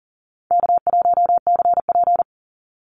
Morse code, heard as a keyed tone, about 40 wpm, 700 Hz. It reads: K1CP
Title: Easy Call Signs - 40wpm